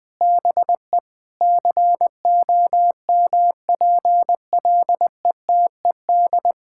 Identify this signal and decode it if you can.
Morse code, heard as a keyed tone, about 21 wpm, 700 Hz. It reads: BE COMPLETED
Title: Sets of 8 Words - 20wpm